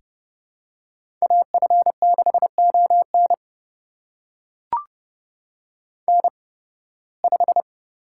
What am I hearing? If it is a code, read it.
Morse code, heard as a keyed tone, about 30 wpm, spaced as for 20 wpm, 700 Hz. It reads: AF6ON N5
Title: USA Call Signs - 8x Character Spacing - 30wpm